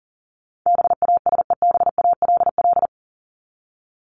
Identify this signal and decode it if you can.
Morse code, heard as a keyed tone, about 40 wpm, 700 Hz. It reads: BASEBALL
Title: Top 1751-2000 Words 40wpm